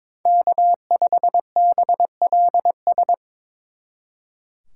Morse code, heard as a keyed tone, about 26 wpm, 700 Hz. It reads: K5BLS
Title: Speed Racing - USA 1x3 Call Signs 22wpm